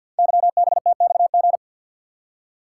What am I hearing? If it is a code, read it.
Morse code, heard as a keyed tone, about 49 wpm, 700 Hz. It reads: YBTXG
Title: Code Groups - 5 Letters - 50wpm